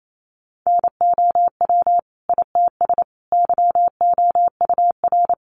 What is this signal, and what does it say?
Morse code, heard as a keyed tone, about 28 wpm, 700 Hz. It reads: NOW ITS YOUR
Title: Sentences from Top 600 Words 28wpm